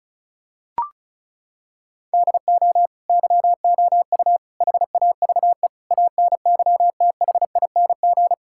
Morse code, heard as a keyed tone, about 34 wpm, 700 Hz. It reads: DO YOU HAVE ANYTHING
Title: Sentences from Top 1000 Words - Review Part 5 - 35wpm